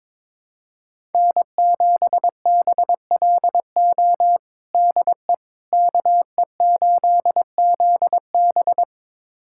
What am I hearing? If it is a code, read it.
Morse code, heard as a keyed tone, about 22 wpm, 700 Hz. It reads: N7BLO DE KE8ZB
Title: USA Callsign DE Callsign - 5 Characters - 22wpm